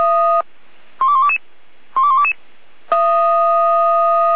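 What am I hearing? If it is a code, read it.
Five-tone selective call (ZVEI), heard as quick runs of five tones, tones of about 70 ms. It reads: 12122 12122
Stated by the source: ZVEI five-tone call received on SDR by Feuerwehr Orlamünde.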